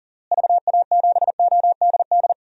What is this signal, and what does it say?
Morse code, heard as a keyed tone, about 40 wpm, 700 Hz. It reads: VA7ODD
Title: DXCC-100 Call Signs - 6 Characters - Repeated 3x - 40wpm